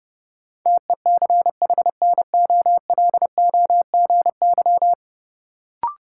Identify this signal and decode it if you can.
Morse code, heard as a keyed tone, about 32 wpm, 700 Hz. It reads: TECHNOLOGY
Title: Sentences from Top 1200 Words - New Word Drill Alt - 30wpm